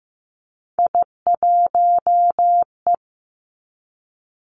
Morse code, heard as a keyed tone, about 15 wpm, 700 Hz. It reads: I1E
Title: Three Challenging Letter-Number Combinations 15wpm